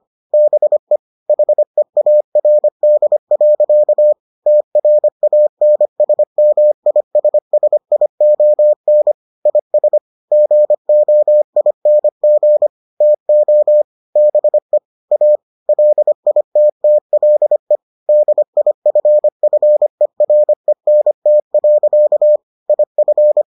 Morse code, heard as a keyed tone, about 25 wpm, 600 Hz. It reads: BE HEARD. TRANSMISSION IS GOING TO BE A LITTLE DIFFERENT. IF